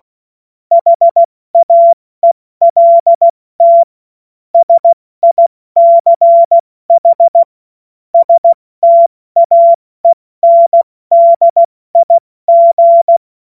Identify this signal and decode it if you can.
Morse code, heard as a keyed tone, about 16 wpm, 680 Hz. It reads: HAELT SICH STAENDIG
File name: Morsekurs / Witze / Witz 05 März 2024 70 F80